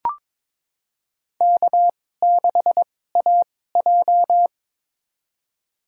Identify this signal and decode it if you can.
Morse code, heard as a keyed tone, about 21 wpm, spaced as for 16 wpm, 700 Hz. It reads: K6AJ
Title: USA Call Signs - 2x Character Spacing - 22wpm